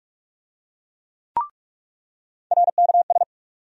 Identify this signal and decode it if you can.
Morse code, heard as a keyed tone, about 44 wpm, 700 Hz. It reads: RKS
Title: USA Call Sign Postfixes - 3 Letters 45wpm